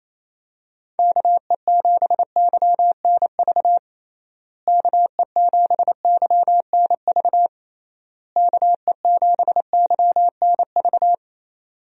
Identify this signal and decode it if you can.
Morse code, heard as a keyed tone, about 28 wpm, 700 Hz. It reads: KE7YNV KE7YNV KE7YNV
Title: USA Call Signs - 6 Characters - Repeated 3x - 28wpm